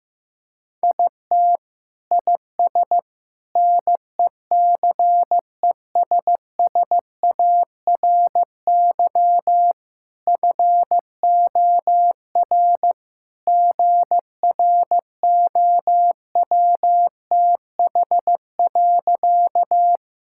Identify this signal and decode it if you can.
Morse code, heard as a keyed tone, about 16 wpm, 690 Hz. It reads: IT IS NECESSARY FOR GROWTH.
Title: Sentences from Top 1000 Words - Review Part 5 - 15wpm